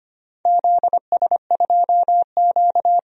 Morse code, heard as a keyed tone, about 25 wpm, 700 Hz. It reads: ZS2Q
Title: DXCC-100 Call Signs - 4 Characters - 25wpm